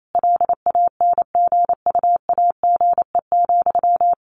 Morse code, heard as a keyed tone, about 28 wpm, 700 Hz. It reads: LANGUAGE,
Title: Sentences from Top 600 Words 28wpm